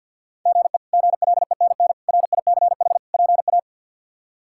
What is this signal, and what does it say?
Morse code, heard as a keyed tone, about 48 wpm, 700 Hz. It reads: GE GLENN RICH PA
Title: K1USN SST QSO Practice - 50wpm